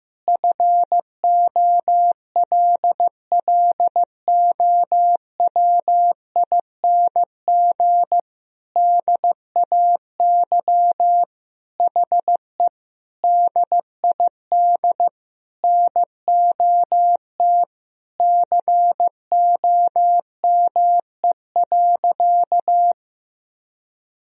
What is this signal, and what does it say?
Morse code, heard as a keyed tone, about 15 wpm, 700 Hz. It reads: FOLLOWING DAY HE DID NOT COME.
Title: Sentences from Top 1000 Words - Review Part 4 - 15wpm